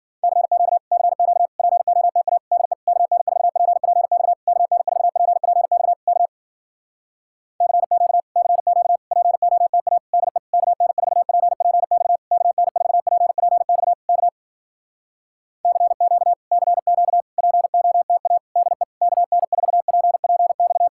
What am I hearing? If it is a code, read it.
Morse code, heard as a keyed tone, about 52 wpm, 700 Hz. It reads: CQ CQ POTA DE KN4PPX KN4PPX K CQ CQ POTA DE KN4PPX KN4PPX K CQ CQ POTA DE KN4PPX